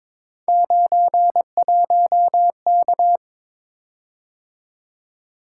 Morse code, heard as a keyed tone, about 22 wpm, 700 Hz. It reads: 91K
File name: practice / 22wpm / Three Letter-Number Combinations - 22wpm